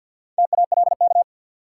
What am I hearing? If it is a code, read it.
Morse code, heard as a keyed tone, about 53 wpm, 700 Hz. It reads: TALK
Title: Sets of 2 Words - New Word Drill - 50wpm